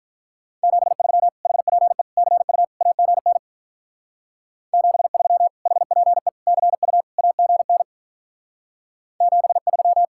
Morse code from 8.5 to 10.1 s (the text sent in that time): 73